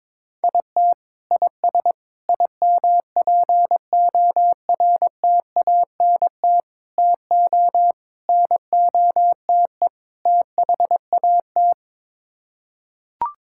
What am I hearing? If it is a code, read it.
Morse code, heard as a keyed tone, about 22 wpm, 700 Hz. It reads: IT IS IMPORTANT TO NOTE THAT
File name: practice / 22wpm / Sets of 6 Words - 22wpm